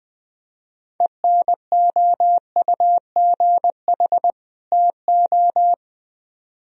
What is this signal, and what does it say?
Morse code, heard as a keyed tone, about 22 wpm, 700 Hz. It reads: ENOUGH TO
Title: Speed-Racing - Sets of 2 Words 20wpm